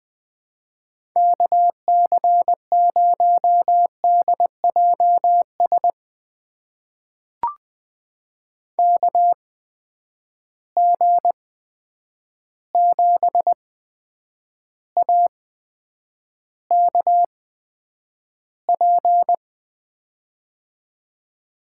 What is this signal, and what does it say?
Morse code, heard as a keyed tone, about 20 wpm, spaced as for 9 wpm, 700 Hz. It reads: KC0DJS KG7AKP
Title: USA Call Signs - 6 Characters - 8x Character Spacing - 20wpm